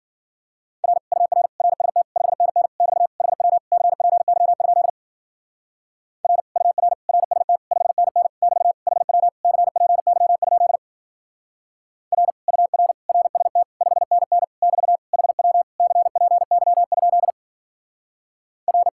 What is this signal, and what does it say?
Morse code, heard as a keyed tone, about 52 wpm, 700 Hz. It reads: R UR RST 5NN <BT> HW CPY? R UR RST 5NN <BT> HW CPY? R UR RST 5NN <BT> HW CPY? R